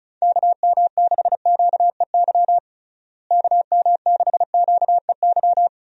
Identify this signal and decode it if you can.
Morse code, heard as a keyed tone, about 36 wpm, 700 Hz. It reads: KM6QEY KM6QEY
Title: USA Call Signs - 6 Characters - Repeated 2x - 35wpm